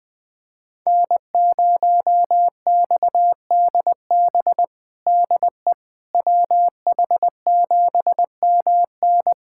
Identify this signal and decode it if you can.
Morse code, heard as a keyed tone, about 20 wpm, 700 Hz. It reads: N0XDB DE WH7MN
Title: USA Callsign DE Callsign - 5 Characters - 20wpm